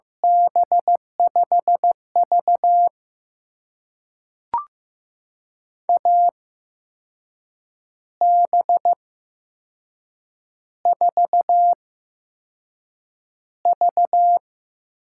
Morse code, heard as a keyed tone, about 15 wpm, spaced as for 6 wpm, 700 Hz. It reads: B5V AB4V
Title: USA Call Signs - 4 Characters - 8x Character Spacing - 15wpm